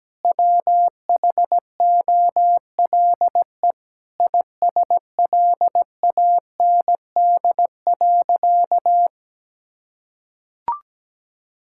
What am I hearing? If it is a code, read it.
Morse code, heard as a keyed tone, about 17 wpm, 700 Hz. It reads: WHOLE ISLAND.
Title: Sentences from Top 1000 Words - Review Part 5 - 17wpm